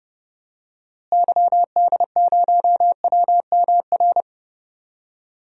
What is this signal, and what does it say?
Morse code, heard as a keyed tone, about 30 wpm, 700 Hz. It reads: YD0WMR
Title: DXCC-100 Call Signs - 6 Characters - Repeated 3x - 30wpm